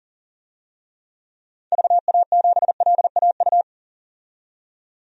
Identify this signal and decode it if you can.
Morse code, heard as a keyed tone, about 40 wpm, 700 Hz. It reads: VA7LAU
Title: DXCC-100 Call Signs - 6 Characters - 40wpm